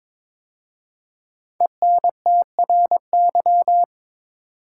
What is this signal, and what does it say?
Morse code, heard as a keyed tone, about 22 wpm, 700 Hz. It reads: ENTRY
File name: 5 Letter Words - First 3 Letters Given - 22wpm